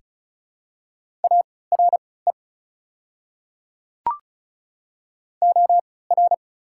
Morse code, heard as a keyed tone, about 35 wpm, spaced as for 20 wpm, 690 Hz. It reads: ARE OR
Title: 3x Character Spacing - Top 100 Words - 35wpm